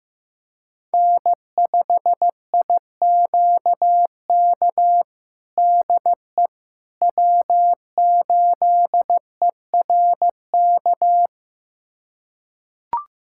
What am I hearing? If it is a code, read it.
Morse code, heard as a keyed tone, about 15 wpm, 700 Hz. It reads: N5IQK DE W8ERK
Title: USA Callsign DE Callsign - 5 Characters - 15wpm